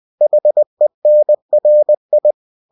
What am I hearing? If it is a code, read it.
Morse code, heard as a keyed tone, about 20 wpm, 600 Hz. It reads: HENRI